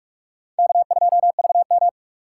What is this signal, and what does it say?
Morse code, heard as a keyed tone, about 44 wpm, 690 Hz. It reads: K2VM
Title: USA Call Signs - 4 Characters - 2x Character Spacing - 45wpm